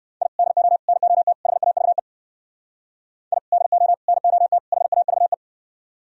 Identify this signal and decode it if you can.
Morse code, heard as a keyed tone, about 53 wpm, 700 Hz. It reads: I DO NOT HAVE I DO NOT HAVE
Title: Speed-Racing - Sets of 4 Words 45wpm